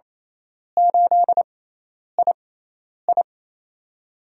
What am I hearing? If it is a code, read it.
Morse code, heard as a keyed tone, about 28 wpm, spaced as for 10 wpm, 700 Hz. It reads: 8II
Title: USA Call Signs - 5 Characters - 6x Character Spacing - 28wpm